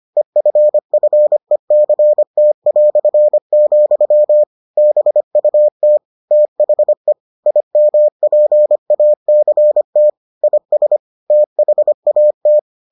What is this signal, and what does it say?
Morse code, heard as a keyed tone, about 25 wpm, 600 Hz. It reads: EFFECT", BUT THE IMPACT IS THAT